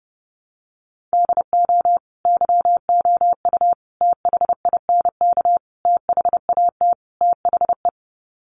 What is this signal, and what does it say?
Morse code, heard as a keyed tone, about 30 wpm, 700 Hz. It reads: DO YOU THINK THAT THE
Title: Sets of 5 Words Encore 30wpm